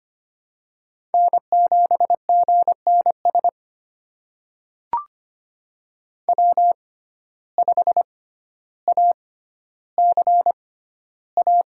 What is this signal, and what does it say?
Morse code, heard as a keyed tone, about 25 wpm, spaced as for 13 wpm, 700 Hz. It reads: N7GNS W5ACA
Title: USA Call Signs - 6x Character Spacing - 25wpm